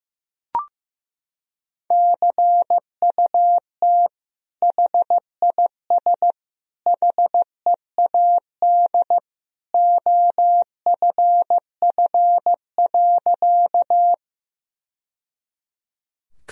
Morse code, heard as a keyed tone, about 15 wpm, 700 Hz. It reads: CUT HIS HEAD OFF.
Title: Sentences from Top 1000 Words - Review Part 1 - 15wpm